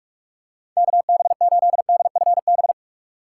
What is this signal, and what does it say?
Morse code, heard as a keyed tone, about 46 wpm, 700 Hz. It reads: KB8DFB
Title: USA Call Signs - 2x Character Spacing - 45wpm